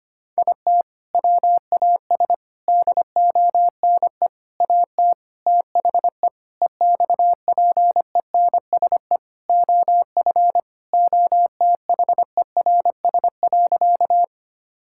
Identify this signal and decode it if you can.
Morse code, heard as a keyed tone, about 25 wpm, 700 Hz. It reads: IT WAS DONE AT THE EXPENSE OF OTHERS.
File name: practice / 25wpm / Sentences with 4-Word Phrases - 25wpm